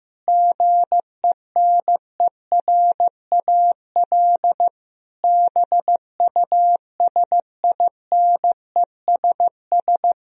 Morse code, heard as a keyed tone, about 15 wpm, 700 Hz. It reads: GENERAL BUSINESS
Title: Sentences from Top 1000 Words - Review Part 3 - 15wpm